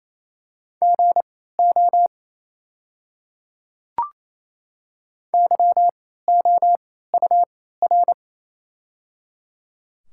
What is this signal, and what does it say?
Morse code, heard as a keyed tone, about 28 wpm, spaced as for 16 wpm, 690 Hz. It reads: GO YOUR
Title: 3x Character Spacing - Top 100 Words - 28wpm